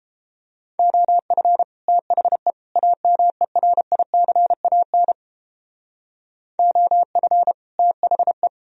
Morse code, heard as a keyed tone, about 29 wpm, 700 Hz. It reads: OF THE AMERICAN OF THE